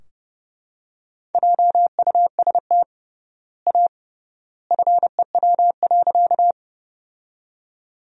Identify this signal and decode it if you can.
Morse code, heard as a keyed tone, about 32 wpm, 700 Hz. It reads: JUST A FEW.
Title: 3x Word-Spacing - Sentences from Top 300 Words - 30wpm